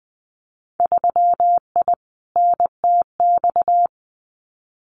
Morse code, heard as a keyed tone, about 20 wpm, 700 Hz. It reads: 3I NTX
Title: Winter Field Day Exchanges 20wpm